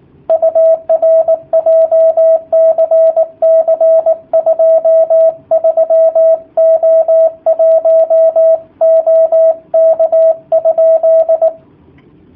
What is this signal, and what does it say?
Morse code, heard as a keyed tone, about 19 wpm, 650 Hz. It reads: UR JCC 23O1 OK?
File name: jh8cbh / 2019cw / sp18- w